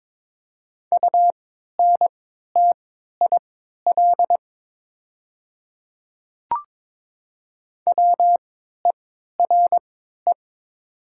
Morse code, heard as a keyed tone, about 22 wpm, spaced as for 12 wpm, 700 Hz. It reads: UNTIL WERE
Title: 3x Character Spacing - Top 300 Words - 22wpm